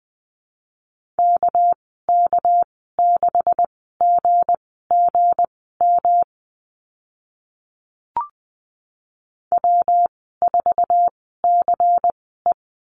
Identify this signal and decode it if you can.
Morse code, heard as a keyed tone, about 20 wpm, spaced as for 14 wpm, 700 Hz. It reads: KK6GGM W4CE
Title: 200 Lesson 32g - Letter k - Call Signs 20wpm